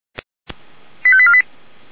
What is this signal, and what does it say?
Five-tone selective call (ZVEI), heard as quick runs of five tones, tones of about 70 ms. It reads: 85848
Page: ZVEI